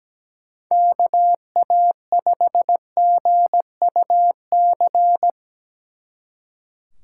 Morse code, heard as a keyed tone, about 17 wpm, 700 Hz. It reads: KA5GUC
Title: Speed Racing - USA 2x3 Call Signs 17wpm